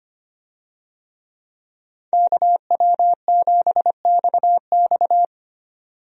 Morse code, heard as a keyed tone, about 25 wpm, 700 Hz. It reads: KW7XX
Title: USA Call Signs - 5 Characters - 25wpm